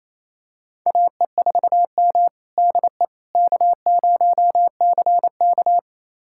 Morse code, heard as a keyed tone, about 28 wpm, 700 Hz. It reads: AE4M DE K0CK
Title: USA Callsign DE Callsign - 4 Characters - 28wpm